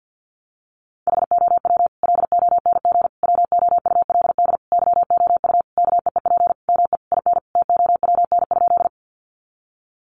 Morse code, heard as a keyed tone, about 50 wpm, 700 Hz. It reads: HOW LONG WOULD YOU KEEP ME IN TOWN?
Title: Sentences from Top 500 Words 50wpm